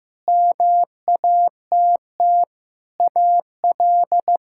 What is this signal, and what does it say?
Morse code, heard as a keyed tone, about 15 wpm, 700 Hz. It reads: MATT AL
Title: K1USN SST QSO Practice - 15wpm